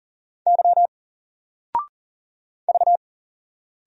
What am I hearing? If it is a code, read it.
Morse code, heard as a keyed tone, about 40 wpm, 700 Hz. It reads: Y V
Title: Single Letter-Number 40wpm